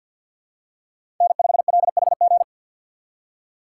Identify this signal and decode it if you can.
Morse code, heard as a keyed tone, about 50 wpm, 700 Hz. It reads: N5LHG
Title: USA Call Signs - 5 Characters - 3x Character Spacing - 50wpm